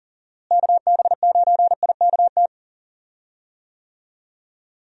Morse code, heard as a keyed tone, about 40 wpm, 700 Hz. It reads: KB9IKT
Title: USA Call Signs - 6 Characters - Repeated 3x - 40wpm